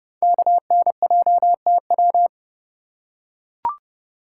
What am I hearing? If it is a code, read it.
Morse code, heard as a keyed tone, about 29 wpm, 700 Hz. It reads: KNJTW
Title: Code Groups - 5 Letter:Numbers - 30wpm